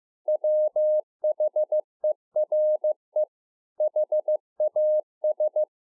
Morse code, heard as a keyed tone, about 15 wpm, 610 Hz. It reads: WHERE HAS
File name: 15WPM